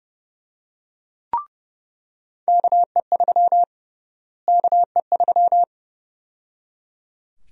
Morse code, heard as a keyed tone, about 30 wpm, 700 Hz. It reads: KE3 KE3
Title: USA Call Sign Prefixes - Repeated 2x - 30wpm